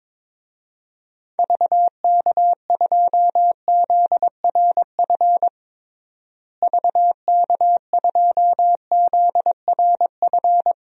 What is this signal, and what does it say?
Morse code, heard as a keyed tone, about 22 wpm, 700 Hz. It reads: VK2ZRF VK2ZRF
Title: DXCC-100 Call Signs - 6 Characters - Repeated 2x - 22wpm